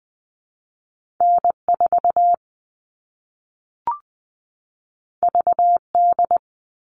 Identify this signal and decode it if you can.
Morse code, heard as a keyed tone, about 20 wpm, 700 Hz. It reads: N4 VD
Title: All Cut Numbers - 2-Digit 20wpm